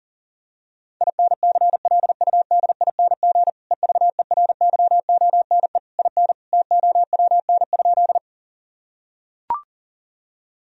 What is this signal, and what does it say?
Morse code, heard as a keyed tone, about 37 wpm, 700 Hz. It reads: INCLUDING EVERYONE IN TOWN?
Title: Sentences from Top 1000 Words - Review Part 4 - 40wpm